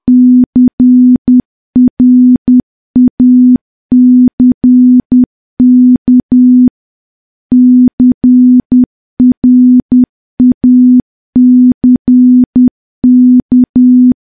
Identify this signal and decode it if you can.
Morse code, heard as a keyed tone, about 10 wpm, 250 Hz. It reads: CRACK CRACK
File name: morse / morse-training-10WPM